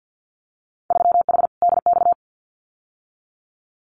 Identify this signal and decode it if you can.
Morse code, heard as a keyed tone, about 50 wpm, 700 Hz. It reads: 3H DX
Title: Winter Field Day Exchanges Encore 50wpm